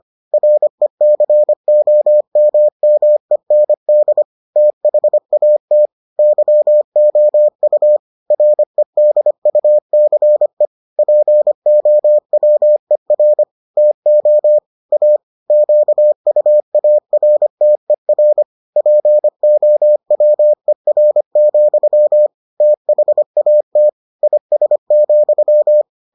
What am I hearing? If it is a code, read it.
Morse code, heard as a keyed tone, about 25 wpm, 600 Hz. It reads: RECOMMEND THAT YOU REDUCE POWER TO A QUARTER POWER, THAT IS,